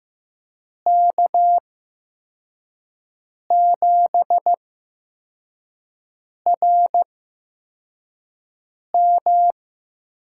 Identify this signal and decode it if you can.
Morse code, heard as a keyed tone, about 15 wpm, spaced as for 4 wpm, 700 Hz. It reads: K7RM
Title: USA Call Signs - 5 Characters - 8x Character Spacing - 15wpm